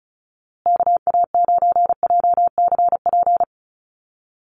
Morse code, heard as a keyed tone, about 34 wpm, 700 Hz. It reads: KA9JCP
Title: USA Call Signs Repeated 2x 35wpm